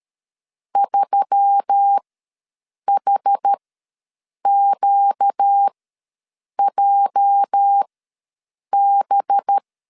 Morse code, heard as a keyed tone, about 13 wpm, spaced as for 7 wpm, 800 Hz. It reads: 3HQJB
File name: LEZIONI / 9 2 ws 12wpm 3406